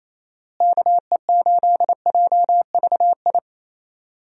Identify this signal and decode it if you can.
Morse code, heard as a keyed tone, about 28 wpm, 700 Hz. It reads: KE8JVI
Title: USA Call Signs - 6 Characters - Repeated 2x - 28wpm